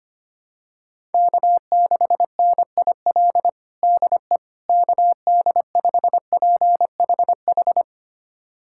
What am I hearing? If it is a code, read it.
Morse code, heard as a keyed tone, about 25 wpm, 700 Hz. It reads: K6NIL DE KD5PHH
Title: USA Callsign DE Callsign 25wpm